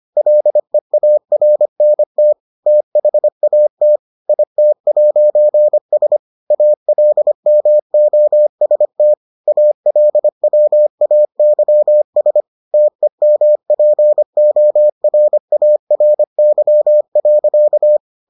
Morse code, heard as a keyed tone, about 25 wpm, 600 Hz. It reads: LEARNT THAT IT'S ALMOST ALWAYS TEMPORARY.